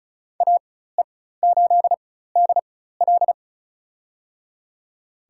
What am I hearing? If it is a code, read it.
Morse code, heard as a keyed tone, about 36 wpm, spaced as for 16 wpm, 700 Hz. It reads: AE8DL
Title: USA Call Signs - 5 Characters - 4x Character Spacing - 35wpm